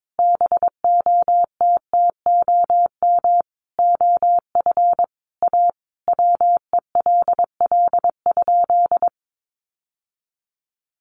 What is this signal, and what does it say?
Morse code, heard as a keyed tone, about 22 wpm, 700 Hz. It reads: BOTTOM OF A WELL?
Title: Sentences from Top 900 Words 22wpm